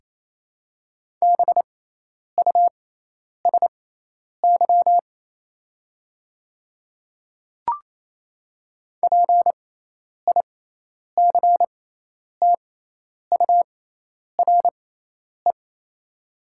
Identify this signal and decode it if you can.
Morse code, heard as a keyed tone, about 28 wpm, spaced as for 10 wpm, 700 Hz. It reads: BUSY PICTURE
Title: 6x Character Spacing - Top 500 Words - 28wpm